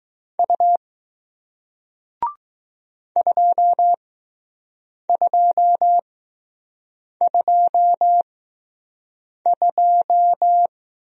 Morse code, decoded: U 2222

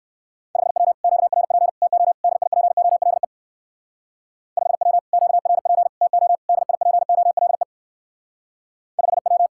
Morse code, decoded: 5W QRP TO DIPOLE 5W QRP TO DIPOLE 5W

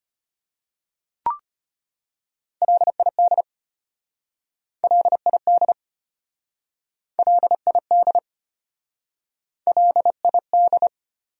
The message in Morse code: LID LID LID LID

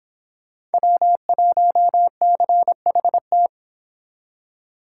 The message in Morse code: W1CHT

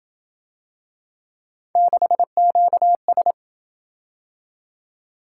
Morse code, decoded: 6QS

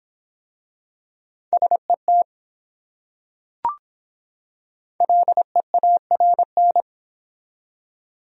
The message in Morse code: SET LEARN

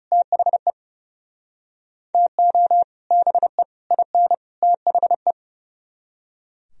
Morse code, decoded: THE TO BE IN THE

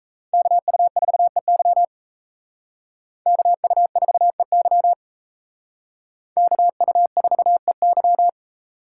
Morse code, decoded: KU4EY KU4EY KU4EY

